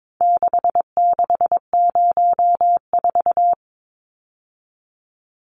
Morse code, heard as a keyed tone, about 22 wpm, 700 Hz. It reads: 6604
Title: Numbers - 4-Digit 22wpm